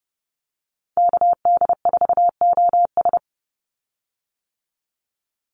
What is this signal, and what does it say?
Morse code, heard as a keyed tone, about 30 wpm, 700 Hz. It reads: KD4OS
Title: Call Signs - 30wpm